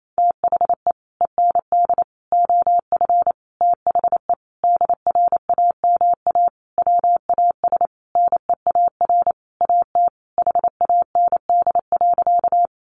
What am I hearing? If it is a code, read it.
Morse code, heard as a keyed tone, about 28 wpm, 700 Hz. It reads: THE END OF THE DRAMA WAS NEAR AT HAND.
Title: Sentences from Top 1500 Words 28wpm